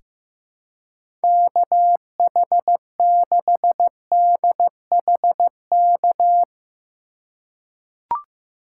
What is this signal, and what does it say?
Morse code, heard as a keyed tone, about 15 wpm, 700 Hz. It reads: KH6DHK
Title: K1USN SST QSO Practice - 15wpm